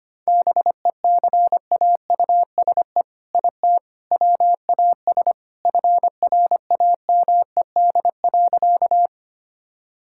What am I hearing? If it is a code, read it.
Morse code, heard as a keyed tone, about 25 wpm, 700 Hz. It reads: BECAUSE IT WAS FRAMED.
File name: practice / 25wpm / All-Age Appropriate Jokes - 25wpm